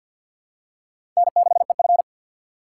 Morse code, heard as a keyed tone, about 53 wpm, 700 Hz. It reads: N6EF
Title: USA Call Signs - 4 Characters - 3x Character Spacing - 50wpm